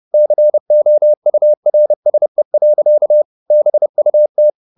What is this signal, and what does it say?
Morse code, heard as a keyed tone, about 30 wpm, 600 Hz. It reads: COURSE. BUT